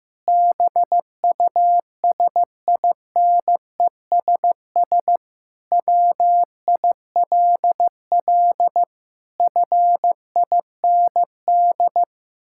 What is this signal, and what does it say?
Morse code, heard as a keyed tone, about 14 wpm, 700 Hz. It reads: BUSINESS WILL FIND
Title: Sentences from Top 1000 Words - Review Part 5 - 15wpm